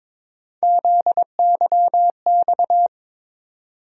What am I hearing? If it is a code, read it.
Morse code, heard as a keyed tone, about 22 wpm, 700 Hz. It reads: ZYX
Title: USA Call Sign Postfixes - 3 Letters 22wpm